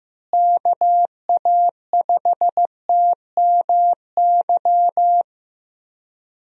Morse code, heard as a keyed tone, about 16 wpm, 700 Hz. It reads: KA5TMY
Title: USA Call Signs - 6 Characters - Repeated 3x - 15wpm